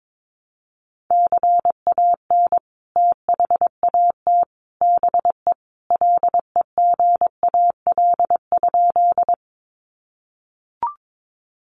Morse code, heard as a keyed tone, about 22 wpm, 700 Hz. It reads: CAN THAT BE LEGAL?
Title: Sentences from Top 1200 Words 22wpm